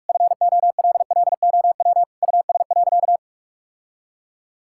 Morse code, heard as a keyed tone, about 46 wpm, 700 Hz. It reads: FOLLOW US.
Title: Sentences from Top 1000 Words - Review Part 6 - 45wpm